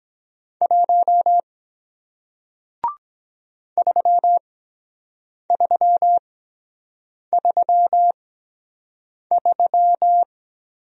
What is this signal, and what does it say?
Morse code, heard as a keyed tone, about 22 wpm, spaced as for 7 wpm, 700 Hz. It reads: 1 3333